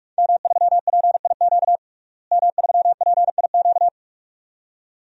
Morse code, heard as a keyed tone, about 46 wpm, 700 Hz. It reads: M3PIQ M3PIQ
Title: DXCC-100 Call Signs - 5 Characters - Repeated 2x - 45wpm